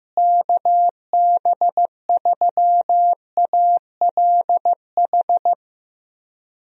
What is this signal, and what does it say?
Morse code, heard as a keyed tone, about 15 wpm, 700 Hz. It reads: KB3ALH
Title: USA Call Signs - 6 Characters - Repeated 3x - 15wpm